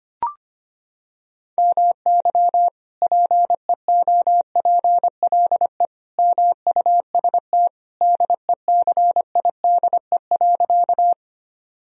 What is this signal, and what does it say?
Morse code, heard as a keyed tone, about 25 wpm, 700 Hz. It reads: MY PEOPLE MUST DECIDE.
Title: Sentences from Top 1000 Words - Review Part 3 - 25wpm